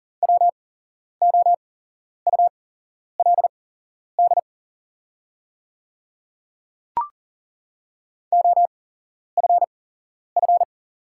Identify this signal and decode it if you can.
Morse code, heard as a keyed tone, about 40 wpm, spaced as for 11 wpm, 700 Hz. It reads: WOULD OFF
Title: 8x Character Spacing - Top 200 Words - 40wpm